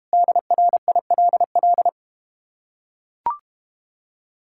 Morse code, heard as a keyed tone, about 32 wpm, 700 Hz. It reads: DRILL